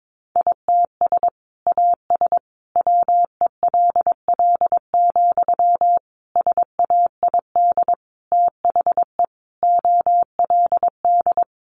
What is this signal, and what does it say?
Morse code, heard as a keyed tone, about 22 wpm, 700 Hz. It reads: ITS AS WELL, SAID THE OLD
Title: Sentences from Top 600 Words 22wpm